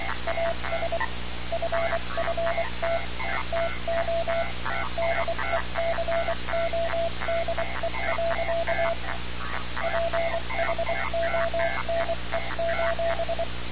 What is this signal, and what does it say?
Morse code, heard as a keyed tone, about 24 wpm, 670 Hz, with noise. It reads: AS FFT TO DECODE. PHONE:
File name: radiocap / REC z